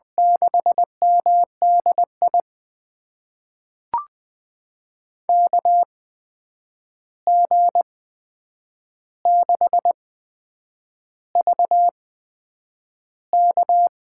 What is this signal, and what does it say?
Morse code, heard as a keyed tone, about 20 wpm, spaced as for 8 wpm, 700 Hz. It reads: 6MDI KG6VK
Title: USA Call Signs - 6 Characters - 8x Character Spacing - 20wpm